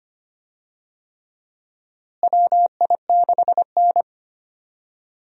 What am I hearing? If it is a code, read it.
Morse code, heard as a keyed tone, about 25 wpm, 700 Hz. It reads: WI6N
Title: USA Call Signs - 4 Characters - Repeated 3x - 25wpm